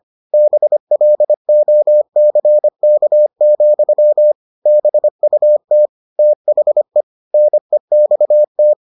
Morse code, heard as a keyed tone, about 25 wpm, 600 Hz. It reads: BLOCK, BUT THE NEXT